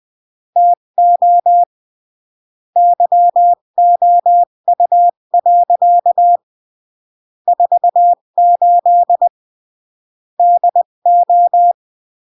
Morse code, heard as a keyed tone, about 20 wpm, 700 Hz. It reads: TO YOU. 48 DO
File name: CW / jodorowsky / 82 commandments 20-15